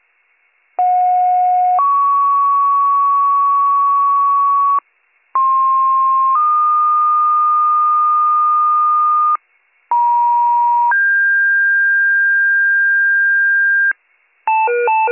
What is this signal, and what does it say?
Two-tone page, heard as a one-second tone, then a three-second tone, about 730 then 1120 Hz, about 1060 then 1220 Hz, about 950 then 1600 Hz. It is heard from 0.8 to 13.9 s.